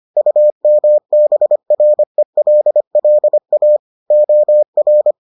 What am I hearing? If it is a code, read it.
Morse code, heard as a keyed tone, about 25 wpm, 600 Hz. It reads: UMBRELLA OR